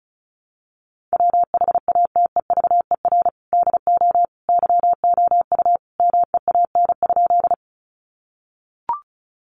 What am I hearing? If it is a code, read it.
Morse code, heard as a keyed tone, about 34 wpm, 700 Hz. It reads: WHATEVER DO YOU MEAN?
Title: Sentences from Top 1100 Words 35wpm